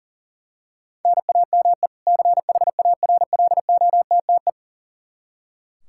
Morse code, heard as a keyed tone, about 40 wpm, 700 Hz. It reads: NAME CHARLOTTE
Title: Practice QSO - Exchange 1 - Beginning - 40wpm